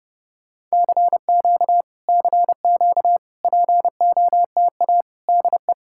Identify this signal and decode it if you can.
Morse code, heard as a keyed tone, about 33 wpm, 700 Hz. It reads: CQ CQ POTA DE